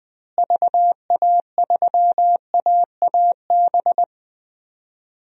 Morse code, heard as a keyed tone, about 20 wpm, 700 Hz. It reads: VA3AAB
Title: DXCC-100 Call Signs - 6 Characters - Repeated 3x - 20wpm